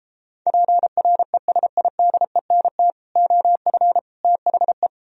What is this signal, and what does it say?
Morse code, heard as a keyed tone, about 34 wpm, 700 Hz. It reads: PRESIDENT OF THE